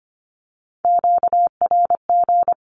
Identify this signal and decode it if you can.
Morse code, heard as a keyed tone, about 25 wpm, 700 Hz. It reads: QRG
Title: Q-Codes with Spoken Meaning - 25wpm